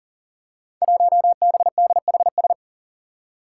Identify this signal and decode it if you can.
Morse code, heard as a keyed tone, about 40 wpm, 700 Hz. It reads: 1BDHS
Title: Five Challenging Letter-Number Combinations 40wpm